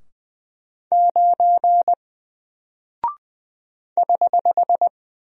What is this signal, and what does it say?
Morse code, heard as a keyed tone, about 20 wpm, 700 Hz. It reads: 9 <HH>
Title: Single Character (Letter:Number:Punctuation:Prosign) Alt 20wpm